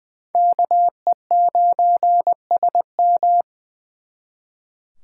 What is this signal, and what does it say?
Morse code, heard as a keyed tone, about 24 wpm, 700 Hz. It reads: KE9SM
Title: Speed Racing - USA 2x2 Call Signs 20wpm